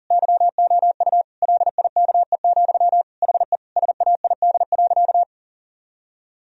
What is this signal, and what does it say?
Morse code, heard as a keyed tone, about 40 wpm, 700 Hz. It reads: YOU LIKE, HE SAID.
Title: Sentences from Top 1000 Words - Review Part 2 - 40wpm